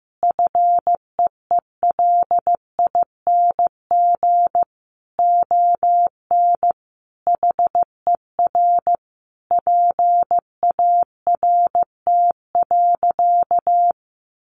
Morse code, heard as a keyed tone, about 14 wpm, 700 Hz. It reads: FEELING ON HER PART.
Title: Sentences from Top 1200 Words 15wpm